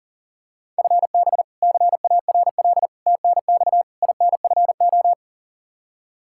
Morse code, heard as a keyed tone, about 40 wpm, 690 Hz. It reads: FB CARL TNX INFO
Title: Practice QSO - Exchange 3 - More Info - 40wpm